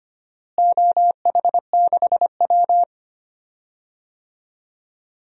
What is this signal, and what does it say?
Morse code, heard as a keyed tone, about 25 wpm, 700 Hz. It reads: OH6W
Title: DXCC-100 Call Signs - 4 Characters - Repeated 2x - 25wpm